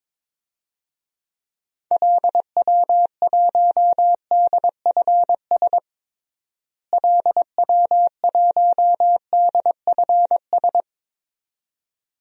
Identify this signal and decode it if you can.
Morse code, heard as a keyed tone, about 21 wpm, 700 Hz. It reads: LW1DFS LW1DFS
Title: DXCC-100 Call Signs - 6 Characters - Repeated 2x - 22wpm